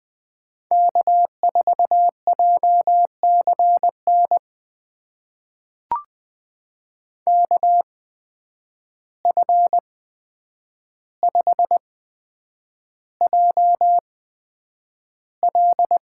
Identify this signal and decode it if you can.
Morse code, heard as a keyed tone, about 20 wpm, spaced as for 9 wpm, 700 Hz. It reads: K4JCN KF5JL
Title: USA Call Signs - 5 Characters - 8x Character Spacing - 20wpm